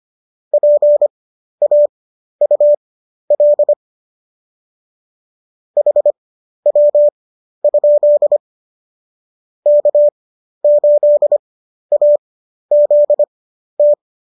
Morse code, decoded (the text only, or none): PAUL HW? K8AZT